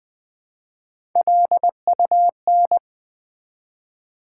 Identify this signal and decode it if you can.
Morse code, heard as a keyed tone, about 20 wpm, 700 Hz. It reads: LUN
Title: Build A Word - 1 Letter At A Time - 3 to 5-Letter Words - 20wpm